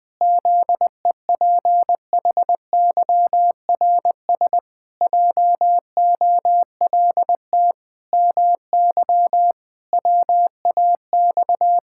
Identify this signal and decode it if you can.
Morse code, heard as a keyed tone, about 20 wpm, 700 Hz. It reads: ZEPHYRS JOLT MY WAX
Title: Pangram Challenge v2 - 20wpm